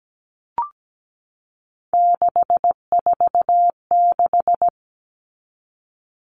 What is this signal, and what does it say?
Morse code, heard as a keyed tone, about 17 wpm, 700 Hz. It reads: 646
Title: All Cut Numbers - 3-Digit 17wpm